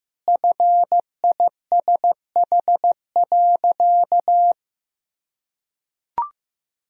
Morse code, heard as a keyed tone, about 15 wpm, 700 Hz. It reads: FISH.
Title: Sentences from Top 1000 Words - Review Part 1 - 15wpm